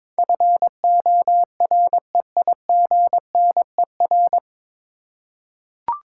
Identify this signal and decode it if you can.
Morse code, heard as a keyed tone, about 22 wpm, 700 Hz. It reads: FOREIGNER
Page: Morse Code Ninja practice 22wpm